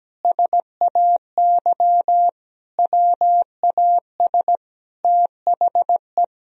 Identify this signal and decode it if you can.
Morse code, heard as a keyed tone, about 17 wpm, 700 Hz. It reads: SAY WAS THE
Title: Sentences from Top 1000 Words - Review Part 1 - 17wpm